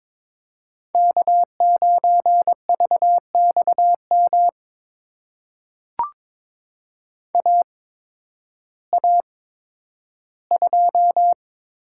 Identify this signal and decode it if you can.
Morse code, heard as a keyed tone, about 22 wpm, spaced as for 12 wpm, 700 Hz. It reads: K9VXM AA2
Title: USA Call Signs - 5 Characters - 8x Character Spacing - 22wpm